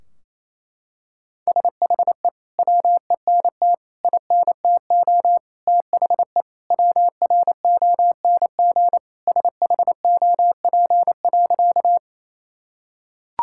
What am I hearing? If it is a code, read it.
Morse code, heard as a keyed tone, about 27 wpm, 700 Hz. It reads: SHE WENT INTO THE WRONG SHOP.
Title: Sentences from Top 1600 Words 28wpm